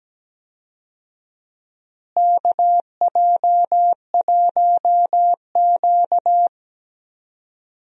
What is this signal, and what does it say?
Morse code, heard as a keyed tone, about 17 wpm, 700 Hz. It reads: KJ1Q
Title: USA Call Signs - 4 Characters - Repeated 2x - 17wpm